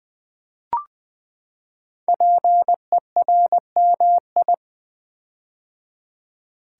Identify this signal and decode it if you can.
Morse code, heard as a keyed tone, about 20 wpm, 700 Hz. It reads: PERMI
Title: Build A Word - 5 Letters At A Time - 10-Letter Words - 20wpm